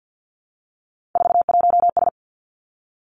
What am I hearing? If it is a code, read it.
Morse code, heard as a keyed tone, about 50 wpm, 700 Hz. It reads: 41S
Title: Three Challenging Letter-Number Combinations 50wpm